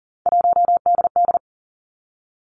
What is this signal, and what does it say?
Morse code, heard as a keyed tone, about 40 wpm, 700 Hz. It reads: 1DD
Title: Three Challenging Letter-Number Combinations 40wpm